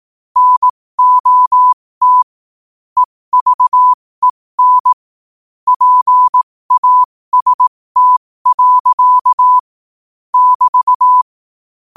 Morse code, heard as a keyed tone, about 18 wpm, spaced as for 15 wpm, 1000 Hz. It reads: NOT EVEN PAST. <BT>
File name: qotdfiles / 15wpm